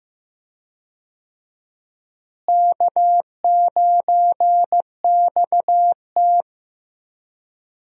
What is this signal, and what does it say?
Morse code, heard as a keyed tone, about 15 wpm, 700 Hz. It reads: K9XT
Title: USA Call Signs - 4 Characters - Repeated 2x - 15wpm